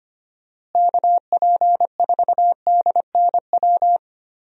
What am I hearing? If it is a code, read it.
Morse code, heard as a keyed tone, about 25 wpm, 700 Hz. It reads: KP4DNW
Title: USA Call Signs - 6 Characters - Repeated 3x - 25wpm